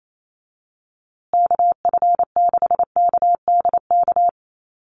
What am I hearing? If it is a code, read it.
Morse code, heard as a keyed tone, about 28 wpm, 700 Hz. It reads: KF6KDK
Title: USA Call Signs Repeated 3x 28wpm